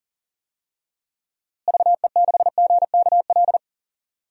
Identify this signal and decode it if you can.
Morse code, heard as a keyed tone, about 38 wpm, 700 Hz. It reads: VE6GKL
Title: DXCC-100 Call Signs - 6 Characters - Repeated 2x - 40wpm